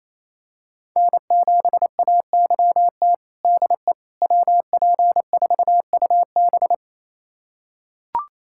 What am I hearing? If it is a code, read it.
Morse code, heard as a keyed tone, about 28 wpm, 700 Hz. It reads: N7AYT DE WP4UB
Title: USA Callsign DE Callsign - 5 Characters - 28wpm